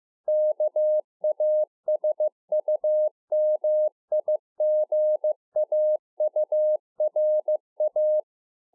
Morse code, heard as a keyed tone, about 15 wpm, 610 Hz. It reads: KASUMIGAURA